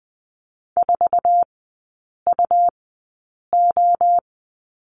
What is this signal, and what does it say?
Morse code, heard as a keyed tone, about 20 wpm, spaced as for 8 wpm, 700 Hz. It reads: 4UO
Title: 166 Lesson 27e - Number 7 - Three Characters 20wpm